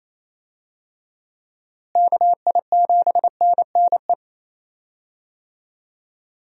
Morse code, heard as a keyed tone, about 28 wpm, 700 Hz. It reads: KI7NNE
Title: USA Call Signs - 6 Characters - Repeated 2x - 28wpm